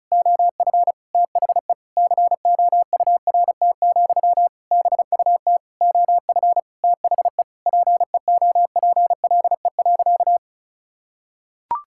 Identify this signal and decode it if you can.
Morse code, heard as a keyed tone, about 36 wpm, 700 Hz. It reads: OF THE COURT, BUT OF THE PEOPLE.
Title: Sentences from Top 1600 Words 35wpm